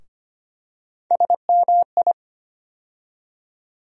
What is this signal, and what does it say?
Morse code, heard as a keyed tone, about 25 wpm, 700 Hz. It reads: SMI
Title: USA Call Sign Postfixes - 3 Letters 25wpm